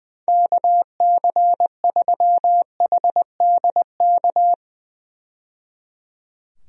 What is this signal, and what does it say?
Morse code, heard as a keyed tone, about 20 wpm, 700 Hz. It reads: KC3HDK
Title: USA Call Signs - 6 Characters - 20wpm